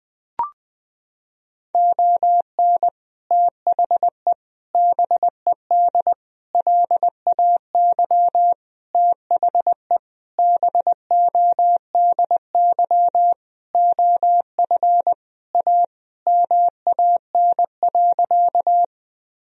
Morse code, heard as a keyed tone, about 20 wpm, 700 Hz. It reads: ON THE BED LAY THE BODY OF A MAN.
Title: Sentences from Top 1000 Words - Review Part 2 - 20wpm